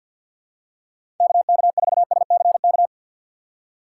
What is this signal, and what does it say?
Morse code, heard as a keyed tone, about 46 wpm, 700 Hz. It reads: KK4SCK
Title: USA Call Signs - 6 Characters - 4x Character Spacing - 50wpm